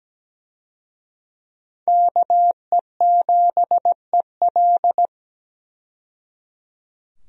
Morse code, heard as a keyed tone, about 17 wpm, 700 Hz. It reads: KE7EL
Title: USA Call Signs - 5 Characters - 17wpm